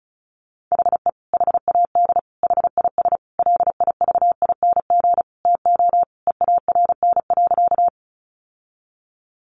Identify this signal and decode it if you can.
Morse code, heard as a keyed tone, about 34 wpm, 700 Hz. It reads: HE HAD HIS LIVING TO EARN.
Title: Sentences from Top 1500 Words 35wpm